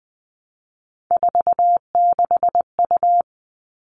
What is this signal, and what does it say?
Morse code, heard as a keyed tone, about 20 wpm, 700 Hz. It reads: 46U
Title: All Cut Numbers - 3-Digit 20wpm